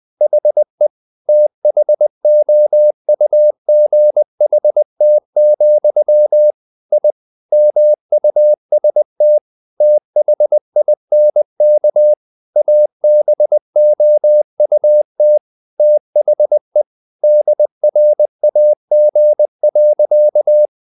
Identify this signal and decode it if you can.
Morse code, heard as a keyed tone, about 20 wpm, 600 Hz. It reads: HE THOUGHT, I MUST THINK ABOUT THE DRAG.
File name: the old man and the sea part 2 20wpm